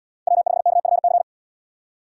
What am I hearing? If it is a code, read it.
Morse code, heard as a keyed tone, about 66 wpm, 700 Hz. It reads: W5KFC